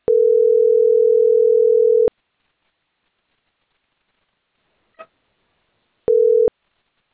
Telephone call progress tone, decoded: ringback tone